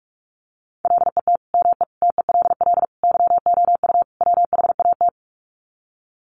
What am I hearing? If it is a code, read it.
Morse code, heard as a keyed tone, about 44 wpm, 700 Hz. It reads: LET ME TELL YOU WHAT
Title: Sets of 5 Words Encore 45wpm